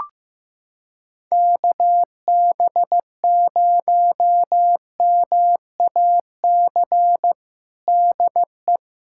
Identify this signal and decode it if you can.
Morse code, heard as a keyed tone, about 15 wpm, 700 Hz. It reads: KB0MAC DE
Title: USA Callsign DE Callsign - 6 Characters - 15wpm